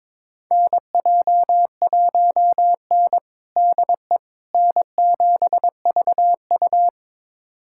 Morse code, heard as a keyed tone, about 22 wpm, 700 Hz. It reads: NJ1N DE N7VU
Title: USA Callsign DE Callsign - 4 Characters - 22wpm